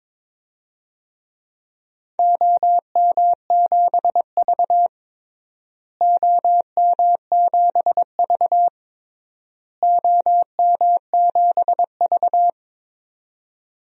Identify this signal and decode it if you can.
Morse code, heard as a keyed tone, about 22 wpm, 700 Hz. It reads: OM7V OM7V OM7V
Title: DXCC-100 Call Signs - 4 Characters - Repeated 3x - 22wpm